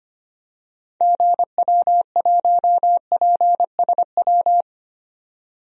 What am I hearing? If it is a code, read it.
Morse code, heard as a keyed tone, about 25 wpm, 700 Hz. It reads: GW1PSW
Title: DXCC-100 Call Signs - 6 Characters - 25wpm